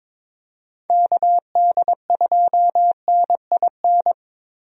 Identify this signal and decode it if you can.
Morse code, heard as a keyed tone, about 22 wpm, 700 Hz. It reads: KD2NIN
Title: USA Call Signs - 6 Characters - Repeated 3x - 22wpm